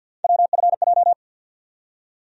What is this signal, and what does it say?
Morse code, heard as a keyed tone, about 52 wpm, 690 Hz. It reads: WFJ
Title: USA Call Sign Postfixes - 3 Letters 50wpm